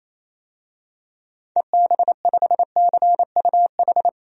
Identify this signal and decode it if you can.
Morse code, heard as a keyed tone, about 28 wpm, 700 Hz. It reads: EB5CUH
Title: DXCC-100 Call Signs - 6 Characters - Repeated 3x - 28wpm